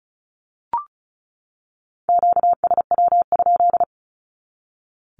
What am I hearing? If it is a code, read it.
Morse code, heard as a keyed tone, about 34 wpm, 700 Hz. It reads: QSW?
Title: Q-Codes 35wpm